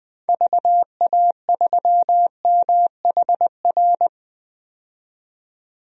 Morse code, heard as a keyed tone, about 20 wpm, 700 Hz. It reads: VA3MHR
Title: DXCC-100 Call Signs - 6 Characters - 20wpm